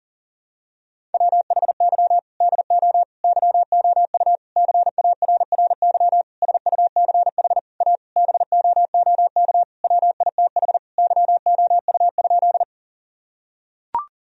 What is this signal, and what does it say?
Morse code, heard as a keyed tone, about 40 wpm, 700 Hz. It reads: WHY DO YOU CARRY SUCH A BOOK WITH YOU?
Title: Sentences from Top 1000 Words - Review Part 1 - 40wpm